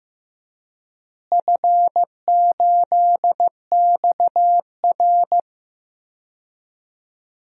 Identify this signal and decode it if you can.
Morse code, heard as a keyed tone, about 15 wpm, 700 Hz. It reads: F8XR
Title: DXCC-100 Call Signs - 4 Characters - Repeated 2x - 15wpm